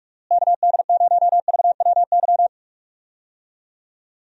Morse code, heard as a keyed tone, about 43 wpm, 700 Hz. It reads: KD0VWY
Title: USA Call Signs - 6 Characters - Repeated 2x - 45wpm